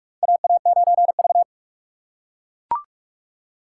Morse code, heard as a keyed tone, about 44 wpm, 700 Hz. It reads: AA9V
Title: USA Call Signs - 4 Characters - 8x Character Spacing - 45wpm